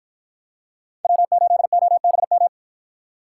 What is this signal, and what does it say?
Morse code, heard as a keyed tone, about 53 wpm, 700 Hz. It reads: W8OBM